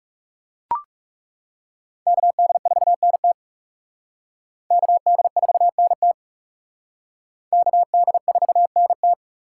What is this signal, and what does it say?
Morse code, heard as a keyed tone, about 40 wpm, 700 Hz. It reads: KD4NT KD4NT KD4NT